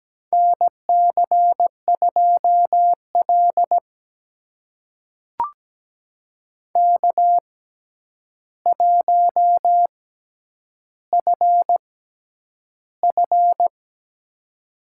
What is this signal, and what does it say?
Morse code, heard as a keyed tone, about 17 wpm, spaced as for 9 wpm, 700 Hz. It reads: NC2L K1FF
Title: USA Call Signs - 4 Characters - 6x Character Spacing - 17wpm